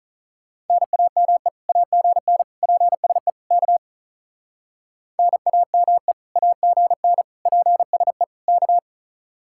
Morse code, heard as a keyed tone, about 44 wpm, 700 Hz. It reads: NAME AGN PSE K NAME AGN PSE K
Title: Speed-Racing Common QSO Phrases - 35wpm